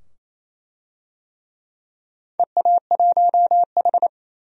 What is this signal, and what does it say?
Morse code, heard as a keyed tone, about 28 wpm, 700 Hz. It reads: EA1H
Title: DXCC-100 Call Signs - 4 Characters - 28wpm